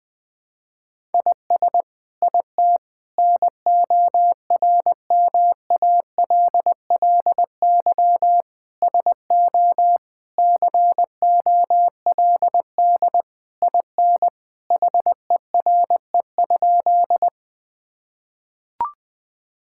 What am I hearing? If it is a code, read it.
Morse code, heard as a keyed tone, about 20 wpm, 700 Hz. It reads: IS IT NORMALLY SO COLD IN HERE?
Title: Sentences from Top 1000 Words - Review Part 6 - 20wpm